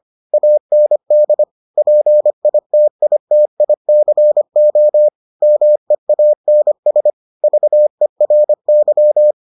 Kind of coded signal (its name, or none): Morse code